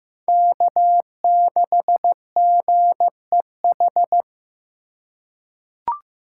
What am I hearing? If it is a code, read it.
Morse code, heard as a keyed tone, about 15 wpm, 700 Hz. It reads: K6GEH
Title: USA Call Signs - 3x Character Spacing - 15wpm